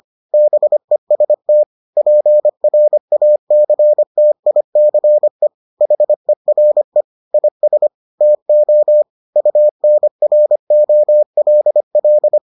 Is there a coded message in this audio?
Yes: Morse code from 0.3 to 12.4 s, heard as a keyed tone, about 25 wpm, 600 Hz. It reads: BEST PRACTICE HERE IS TO UNROLL